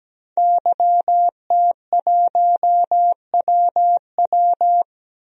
Morse code, heard as a keyed tone, about 17 wpm, 700 Hz. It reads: YT1WW
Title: DXCC-100 Call Signs - 5 Characters - 17wpm